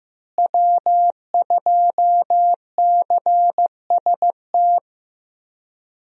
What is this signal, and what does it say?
Morse code, heard as a keyed tone, about 15 wpm, 700 Hz. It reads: W2CST
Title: USA Call Signs - 5 Characters - Repeated 2x - 15wpm